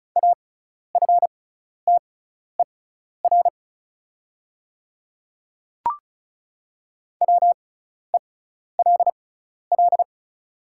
Morse code, heard as a keyed tone, about 36 wpm, spaced as for 12 wpm, 690 Hz. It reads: AFTER WELL
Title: 6x Character Spacing - Top 200 Words - 35wpm